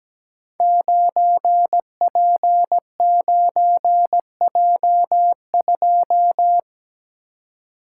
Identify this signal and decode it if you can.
Morse code, heard as a keyed tone, about 17 wpm, 700 Hz. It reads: 9P9J2
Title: Code Groups - 5 Letter:Numbers - 17wpm